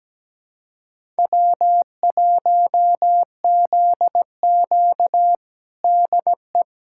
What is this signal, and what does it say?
Morse code, heard as a keyed tone, about 17 wpm, 700 Hz. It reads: W1ZQ DE
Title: USA Callsign DE Callsign - 4 Characters - 17wpm